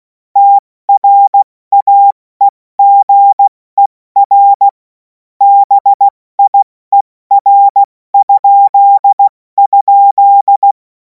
Morse code, heard as a keyed tone, about 16 wpm, 800 Hz. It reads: TRAEGER BIER??
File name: Morsekurs / Witze / Witz 07. April 2026 70 F80